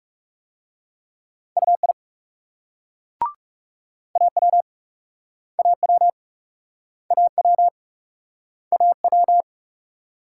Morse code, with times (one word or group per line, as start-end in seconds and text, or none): 1.6-1.9 UI
4.2-4.6 AW
5.6-6.1 AW
7.1-7.7 AW
8.7-9.4 AW